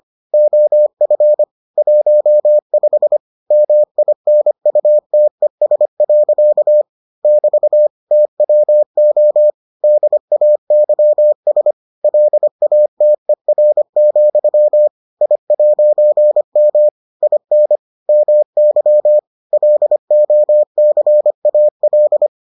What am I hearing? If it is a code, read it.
Morse code, heard as a keyed tone, about 25 wpm, 600 Hz. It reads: OF 15 MINUTES. <BT> TWO DAYS LATER, I'M IN MY LOCAL